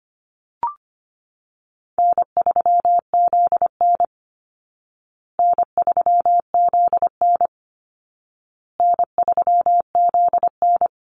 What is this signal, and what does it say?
Morse code, heard as a keyed tone, about 25 wpm, 700 Hz. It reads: N3ZN N3ZN N3ZN
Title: Easy Call Signs Repeated 3x - 25wpm